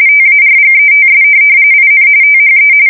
RTTY, about 45.45 baud. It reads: TARGETS IN THE